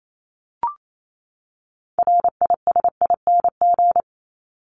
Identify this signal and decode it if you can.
Morse code, heard as a keyed tone, about 28 wpm, 700 Hz. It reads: RISING
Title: ING Suffix - Top 100 Words with Base Words 28wpm